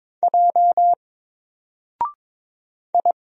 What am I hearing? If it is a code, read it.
Morse code, heard as a keyed tone, about 22 wpm, 700 Hz. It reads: J I
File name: Single-Letter-Number-22wpm